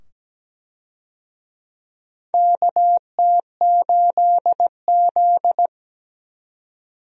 Morse code, heard as a keyed tone, about 17 wpm, 700 Hz. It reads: KT8Z
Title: USA Call Signs - 4 Characters - 17wpm